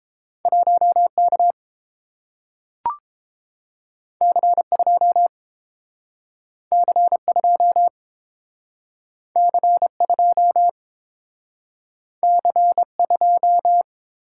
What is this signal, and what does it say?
Morse code, heard as a keyed tone, about 30 wpm, 700 Hz. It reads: 1K C2 C2 C2 C2